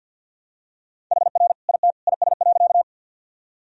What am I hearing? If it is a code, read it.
Morse code, heard as a keyed tone, about 50 wpm, 700 Hz. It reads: HR IT IS.
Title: Sentences from Top 200 Words and Abbreviations 50wpm